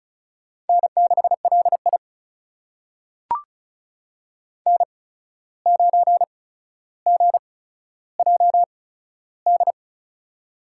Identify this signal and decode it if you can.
Morse code, heard as a keyed tone, about 36 wpm, spaced as for 14 wpm, 700 Hz. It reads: N6LI N9GJD
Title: USA Call Signs - 8x Character Spacing - 35wpm